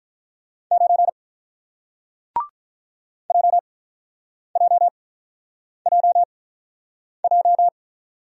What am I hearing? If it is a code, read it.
Morse code, heard as a keyed tone, about 47 wpm, spaced as for 9 wpm, 700 Hz. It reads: 9 JJJJ